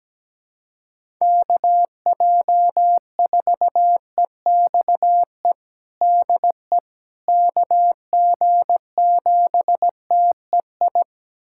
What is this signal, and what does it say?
Morse code, heard as a keyed tone, about 17 wpm, 700 Hz. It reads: KJ4EXE DE KG7TEI
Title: USA Callsign DE Callsign - 6 Characters - 17wpm